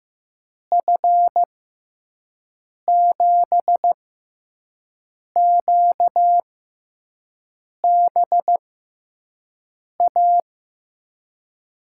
Morse code, heard as a keyed tone, about 15 wpm, 700 Hz. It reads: F 7 Q B A
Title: USA Call Signs - 6x Character Spacing - 15wpm